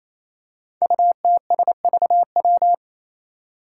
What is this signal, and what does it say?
Morse code, heard as a keyed tone, about 27 wpm, 700 Hz. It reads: UTSVW
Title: Five Challenging Letter-Number Combinations 28wpm